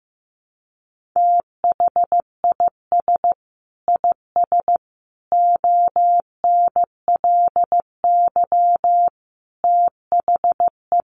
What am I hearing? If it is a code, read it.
Morse code, heard as a keyed tone, about 15 wpm, 700 Hz. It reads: THIS IS ONLY THE
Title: Sentences from Top 800 Words 15wpm